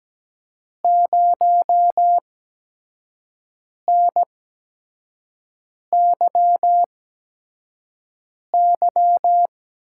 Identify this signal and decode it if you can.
Morse code, heard as a keyed tone, about 17 wpm, spaced as for 5 wpm, 700 Hz. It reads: 0NYY
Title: USA Call Signs - 5 Characters - 8x Character Spacing - 17wpm